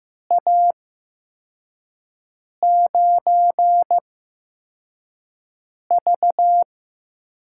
Morse code, heard as a keyed tone, about 15 wpm, spaced as for 4 wpm, 700 Hz. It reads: A9V
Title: USA Call Signs - 5 Characters - 8x Character Spacing - 15wpm